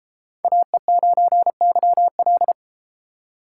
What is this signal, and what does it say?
Morse code, heard as a keyed tone, about 32 wpm, 700 Hz. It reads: AE9YL